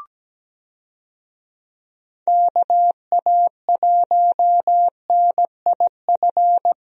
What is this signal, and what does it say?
Morse code, heard as a keyed tone, about 17 wpm, 700 Hz. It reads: KA1NIF
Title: USA Call Signs - 6 Characters - Repeated 3x - 17wpm